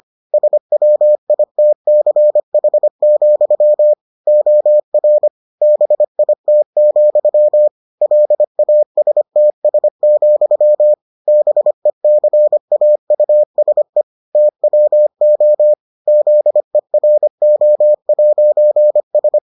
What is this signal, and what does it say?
Morse code, heard as a keyed tone, about 25 wpm, 600 Hz. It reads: SWITCH, OR BIT, LASTS, BECAUSE TWO ZERO'S